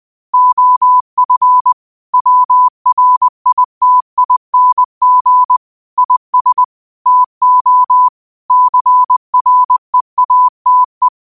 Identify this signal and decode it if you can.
Morse code, heard as a keyed tone, about 20 wpm, 1000 Hz. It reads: OF WRITING IS TO CREATE
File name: qotdfiles / 20wpm